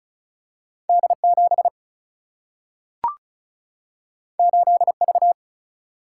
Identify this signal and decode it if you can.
Morse code, heard as a keyed tone, about 36 wpm, 700 Hz. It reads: D7 8V
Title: Single-Letter Single-Number 35wpm